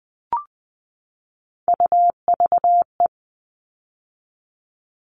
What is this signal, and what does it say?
Morse code, heard as a keyed tone, about 20 wpm, 700 Hz. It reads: UVE
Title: All Cut Numbers - 3-Digit 20wpm